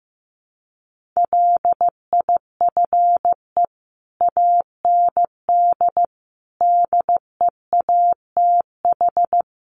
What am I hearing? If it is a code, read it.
Morse code, heard as a keyed tone, about 14 wpm, 700 Hz. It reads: LIFE AND DEATH
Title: Sentences from Top 1100 Words 15wpm